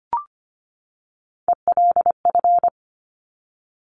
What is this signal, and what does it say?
Morse code, heard as a keyed tone, about 25 wpm, 700 Hz. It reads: ELF
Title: Top Three Letter Combinations in Words 25wpm